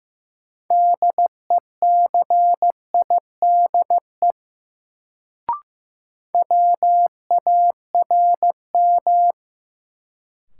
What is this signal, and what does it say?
Morse code, heard as a keyed tone, about 15 wpm, 700 Hz. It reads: DECIDE WARM
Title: Sentences from Top 500 Words - New Word Drill - 15wpm